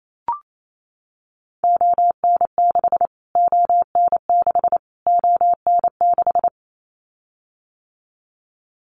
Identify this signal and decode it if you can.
Morse code, heard as a keyed tone, about 28 wpm, 700 Hz. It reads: ON6 ON6 ON6
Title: Callsign Prefixes for Top 25 DXCC - Repeated 3x - 28wpm